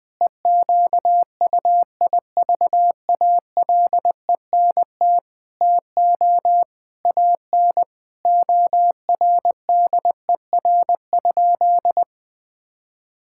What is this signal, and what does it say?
Morse code, heard as a keyed tone, about 20 wpm, 700 Hz. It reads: EQUIVALENT TO AN ORDER?
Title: Sentences from Top 1600 Words 20wpm